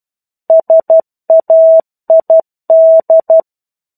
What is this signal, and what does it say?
Morse code, heard as a keyed tone, about 12 wpm, 650 Hz. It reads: SAID
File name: 01-Chapter-01 12 WPM